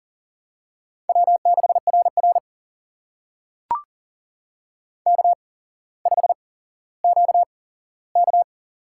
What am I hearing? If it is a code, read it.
Morse code, heard as a keyed tone, about 42 wpm, spaced as for 17 wpm, 700 Hz. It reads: W6RR K5QK
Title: USA Call Signs - 4 Characters - 8x Character Spacing - 40wpm